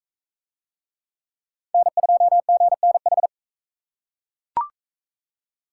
Morse code, heard as a keyed tone, about 40 wpm, 700 Hz. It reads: N2GNH